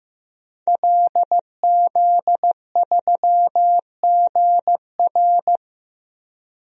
Morse code, heard as a keyed tone, about 15 wpm, 700 Hz. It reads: LZ3GR
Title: DXCC-100 Call Signs - 5 Characters - Repeated 3x - 15wpm